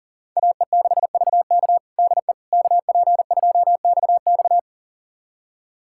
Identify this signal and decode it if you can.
Morse code, heard as a keyed tone, about 37 wpm, 700 Hz. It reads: AE6VK DE KP2XX
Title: USA Callsign DE Callsign - 5 Characters - 40wpm